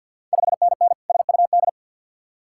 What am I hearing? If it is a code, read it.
Morse code, heard as a keyed tone, about 50 wpm, 700 Hz. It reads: 5NN SUD
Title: Ontario QSO Party Exchange 50wpm